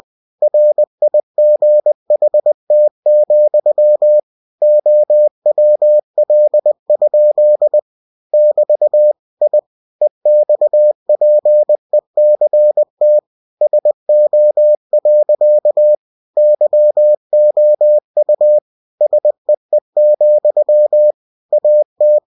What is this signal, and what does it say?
Morse code, heard as a keyed tone, about 20 wpm, 600 Hz. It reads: RIGHT, OWL? <BT> I EXPECT SO. YOU SEE, AT
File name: cw-books / winnie-the-pooh / winnie-20wpm-page-0097